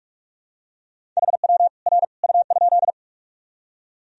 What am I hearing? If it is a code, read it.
Morse code, heard as a keyed tone, about 45 wpm, 700 Hz. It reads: HW R U?
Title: Sentences from Top 100 Words and Abbreviations 45wpm